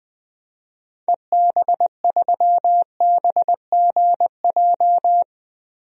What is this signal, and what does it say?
Morse code, heard as a keyed tone, about 20 wpm, 700 Hz. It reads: EB3BGJ
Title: DXCC-100 Call Signs - 6 Characters - Repeated 3x - 20wpm